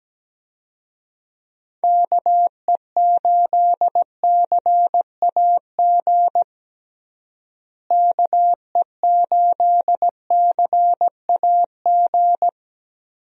Morse code, heard as a keyed tone, about 17 wpm, 700 Hz. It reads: KE8CAG KE8CAG
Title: USA Call Signs - 6 Characters - Repeated 2x - 17wpm